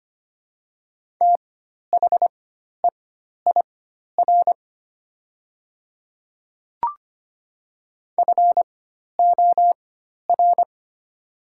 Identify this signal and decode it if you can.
Morse code, heard as a keyed tone, about 25 wpm, spaced as for 12 wpm, 690 Hz. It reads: THEIR FOR
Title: 4x Character Spacing - Top 200 Words - 25wpm